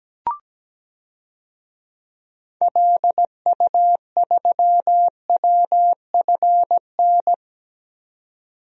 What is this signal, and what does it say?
Morse code, heard as a keyed tone, about 17 wpm, 700 Hz. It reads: LU3WFN
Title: DXCC-100 Call Signs - 6 Characters - 17wpm